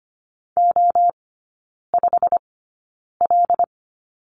Morse code, heard as a keyed tone, about 25 wpm, spaced as for 9 wpm, 690 Hz. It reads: O5L
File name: Lesson 15e - Number 5 - Three Characters 25wpm